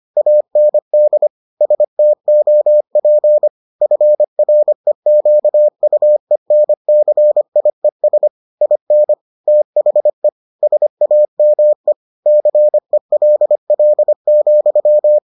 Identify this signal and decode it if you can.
Morse code, heard as a keyed tone, about 25 wpm, 600 Hz. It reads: AND STOP FREQUENCIES IN THE SAME CELL,